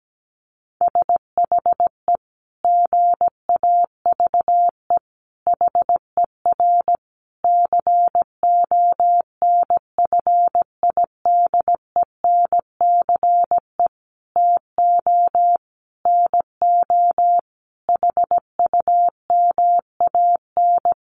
Morse code, heard as a keyed tone, about 17 wpm, 700 Hz. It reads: SHE GAVE HER CONFIDENCE TO NO HUMAN
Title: Sentences from Top 1500 Words 17wpm